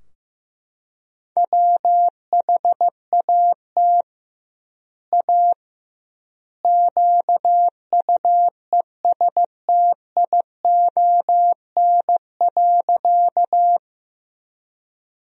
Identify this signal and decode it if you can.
Morse code, heard as a keyed tone, about 15 wpm, 700 Hz. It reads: WHAT A QUESTION.
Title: 2x Word-Spacing - Sentences from Top 400 Words - 15wpm